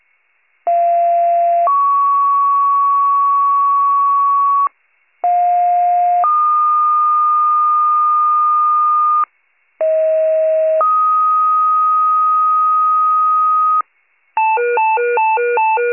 Two-tone page, heard as a one-second tone, then a three-second tone, about 690 then 1120 Hz, about 710 then 1220 Hz, about 640 then 1250 Hz. It is heard from 0.7 to 13.8 s.